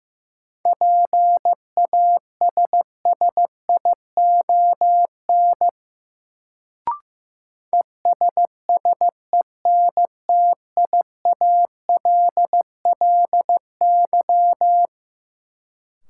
Morse code, heard as a keyed tone, about 15 wpm, 700 Hz. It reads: PASSION ESSENTIALLY
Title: Sentences from Top 1300 Words - New Word Drill - 15wpm